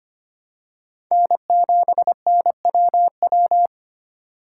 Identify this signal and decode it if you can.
Morse code, heard as a keyed tone, about 25 wpm, 700 Hz. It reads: N7NWW
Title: USA Call Signs - 5 Characters - 4x Character Spacing - 25wpm